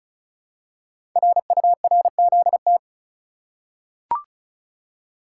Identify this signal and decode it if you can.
Morse code, heard as a keyed tone, about 35 wpm, 700 Hz. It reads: RURZT
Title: Code Groups - 5 Letters - 35wpm